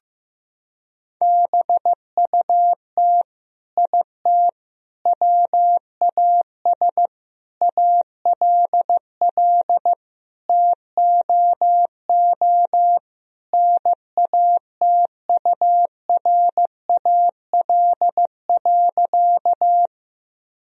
Morse code, heard as a keyed tone, about 15 wpm, 700 Hz. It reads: BUT IT WAS ALL TOO NATURAL.
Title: Sentences from Top 1000 Words - Review Part 4 - 15wpm